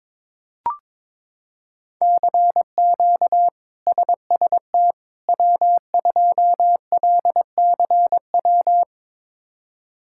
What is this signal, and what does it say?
Morse code, heard as a keyed tone, about 21 wpm, 700 Hz. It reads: CQ SST W2LCW
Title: K1USN SST QSO Practice - 22wpm